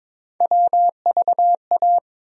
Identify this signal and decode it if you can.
Morse code, heard as a keyed tone, about 22 wpm, 700 Hz. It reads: WVA
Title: USA Call Sign Postfixes - 3 Letters 22wpm